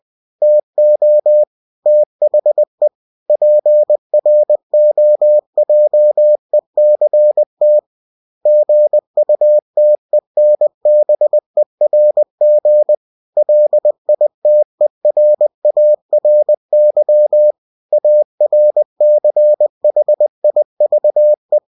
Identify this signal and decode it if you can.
Morse code, heard as a keyed tone, about 20 wpm, 600 Hz. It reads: TO THE PROJECT GUTENBERG LITERARY ARCHIVE
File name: cw-books / winnie-the-pooh / winnie-20wpm-page-0117